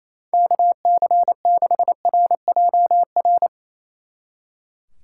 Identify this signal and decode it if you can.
Morse code, heard as a keyed tone, about 36 wpm, 700 Hz. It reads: KC6RJR